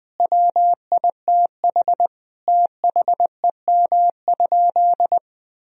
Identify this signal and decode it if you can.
Morse code, heard as a keyed tone, about 20 wpm, 700 Hz. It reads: WITH THEM?
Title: Sentences from Top 1000 Words - Review Part 3 - 20wpm